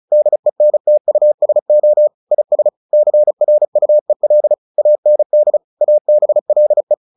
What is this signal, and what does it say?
Morse code, heard as a keyed tone, about 34 wpm, 600 Hz. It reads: DENTUSO IS CRUEL AND ABLE
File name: the old man and the sea 2 35wpm